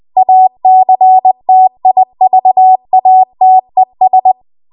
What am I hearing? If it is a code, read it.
Morse code, heard as a keyed tone, about 20 wpm, 750 Hz. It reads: ACTIVATES
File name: Archive / 20 WPM / 20WPM